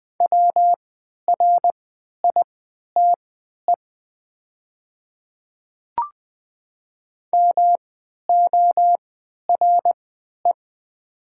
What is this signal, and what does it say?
Morse code, heard as a keyed tone, about 20 wpm, spaced as for 11 wpm, 700 Hz. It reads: WRITE MORE
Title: 3x Character Spacing - Top 100 Words - 20wpm